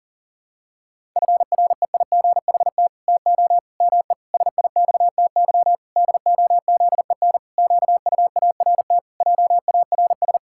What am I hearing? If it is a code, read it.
Morse code, heard as a keyed tone, about 40 wpm, 700 Hz. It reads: FREIGHT TO ME SIXTY DOZEN QUART JARS
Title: Pangram Challenge v2 - 40wpm